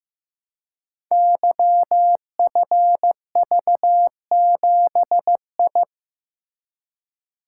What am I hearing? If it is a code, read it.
Morse code, heard as a keyed tone, about 15 wpm, 700 Hz. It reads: YFV7I
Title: Code Groups - 5 Letter:Numbers - 15wpm